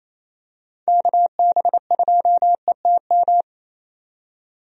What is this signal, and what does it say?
Morse code, heard as a keyed tone, about 28 wpm, 700 Hz. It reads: KB2ETM
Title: USA Call Signs - 6 Characters - 28wpm